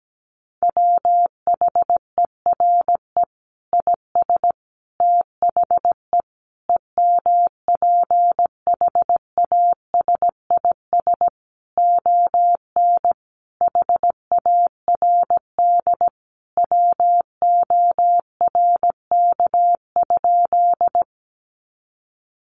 Morse code, heard as a keyed tone, about 17 wpm, 700 Hz. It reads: WHERE IS THE EMPHASIS ON HARD WORK?
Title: Sentences from Top 1400 Words 17wpm